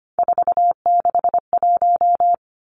Morse code, heard as a keyed tone, about 25 wpm, 700 Hz. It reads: 461
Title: Three Challenging Letter-Number Combinations 25wpm